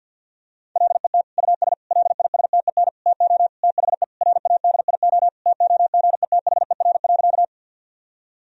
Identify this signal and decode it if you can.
Morse code, heard as a keyed tone, about 50 wpm, 700 Hz. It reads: LET US LISTEN TO THE RADIO TOGETHER.
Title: Sentences from Top 1000 Words - Review Part 3 - 50wpm